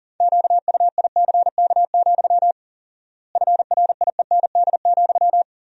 Morse code, heard as a keyed tone, about 40 wpm, 700 Hz. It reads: QUICK, FRIEND,
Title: 4x Word-Spacing - Sentences from Top 500 Words - 40wpm